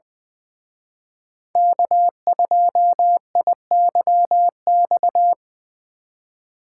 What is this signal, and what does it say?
Morse code, heard as a keyed tone, about 23 wpm, 700 Hz. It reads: K2IYX
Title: Speed Racing - USA 1x3 Call Signs 20wpm